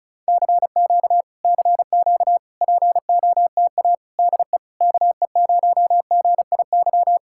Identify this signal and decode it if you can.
Morse code, heard as a keyed tone, about 40 wpm, 700 Hz. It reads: CQ CQ POTA DE KE0GIY